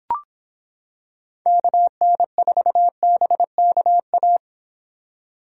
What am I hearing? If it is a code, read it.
Morse code, heard as a keyed tone, about 26 wpm, 700 Hz. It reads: KN4BKA